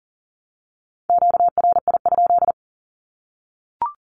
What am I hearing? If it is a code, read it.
Morse code, heard as a keyed tone, about 40 wpm, 700 Hz. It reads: QRI?
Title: Q-Codes 40wpm